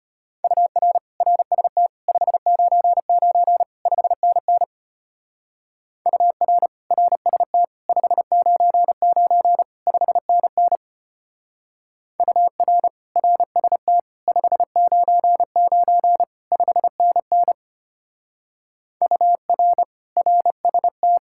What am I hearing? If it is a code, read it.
Morse code, heard as a keyed tone, about 29 wpm, 700 Hz. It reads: UR RST 599 5NN UR RST 599 5NN UR RST 599 5NN UR RST
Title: Speed-Racing Common QSO Phrases Encore - 25wpm